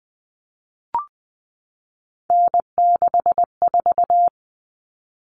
Morse code, heard as a keyed tone, about 20 wpm, 700 Hz. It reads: N64
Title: Common Cut Numbers - 3-Digit 20wpm